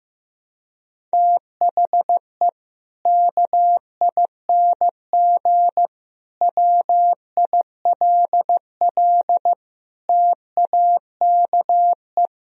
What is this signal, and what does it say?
Morse code, heard as a keyed tone, about 15 wpm, 700 Hz. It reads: THE KING WILL TAKE
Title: Sentences from Top 1000 Words - Review Part 4 - 15wpm